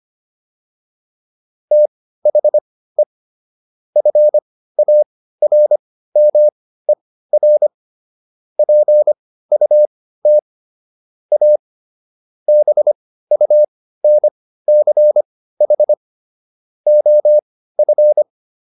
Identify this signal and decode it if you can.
Morse code, heard as a keyed tone, about 25 wpm, spaced as for 15 wpm, 600 Hz. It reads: THE FARMER PUT A BUNCH OF